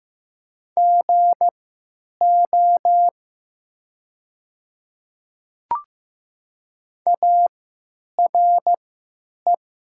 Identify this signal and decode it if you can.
Morse code, heard as a keyed tone, about 15 wpm, spaced as for 9 wpm, 700 Hz. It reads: GO ARE
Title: 3x Character Spacing - Top 300 Words - 15wpm